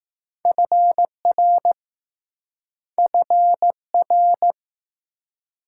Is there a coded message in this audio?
Yes: Morse code from 0.5 to 4.5 s, heard as a keyed tone, about 20 wpm, 700 Hz. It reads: FR FR